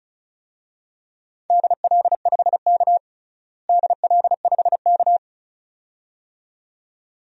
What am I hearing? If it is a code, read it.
Morse code, heard as a keyed tone, about 36 wpm, 700 Hz. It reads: DL5K DL5K
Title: DXCC-100 Call Signs - 4 Characters - Repeated 2x - 35wpm